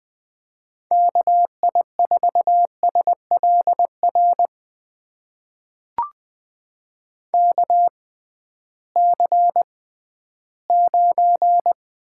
Morse code, heard as a keyed tone, about 20 wpm, spaced as for 13 wpm, 700 Hz. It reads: KI4SLR KC9
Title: USA Call Signs - 6 Characters - 6x Character Spacing - 20wpm